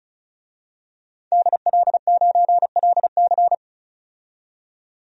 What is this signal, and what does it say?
Morse code, heard as a keyed tone, about 34 wpm, 700 Hz. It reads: DL9LC
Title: DXCC-100 Call Signs - 5 Characters - 35wpm